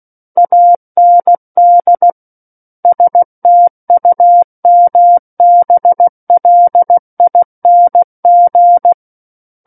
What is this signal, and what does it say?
Morse code, heard as a keyed tone, about 16 wpm, 700 Hz. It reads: AND STUMBLING
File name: Book One / 07-Chapter 7 - 16 WPM